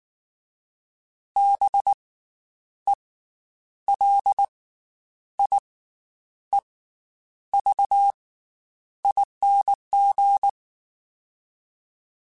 Morse code, heard as a keyed tone, about 19 wpm, 800 Hz. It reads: B E L I E V ING
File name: Special INGs / Special INGs-mp3s / Slow ing @18